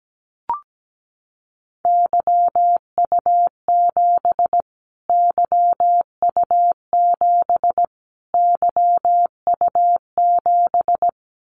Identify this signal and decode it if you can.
Morse code, heard as a keyed tone, about 17 wpm, 700 Hz. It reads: YU7 YU7 YU7
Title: Callsign Prefixes for Top 25 DXCC - Repeated 3x - 17wpm